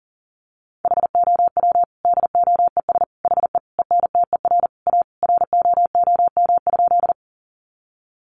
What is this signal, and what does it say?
Morse code, heard as a keyed tone, about 40 wpm, 700 Hz. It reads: HOW DOES HE ENTER A ROOM?
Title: Sentences from Top 1300 Words 40wpm